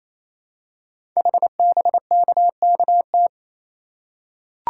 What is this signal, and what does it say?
Morse code, heard as a keyed tone, about 28 wpm, 700 Hz. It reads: HBKKT
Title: Code Groups - 5 Letters - 28wpm